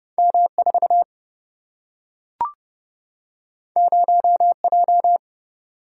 Morse code, heard as a keyed tone, about 30 wpm, 700 Hz. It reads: M4 0J
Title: Single-Letter Single-Number 30wpm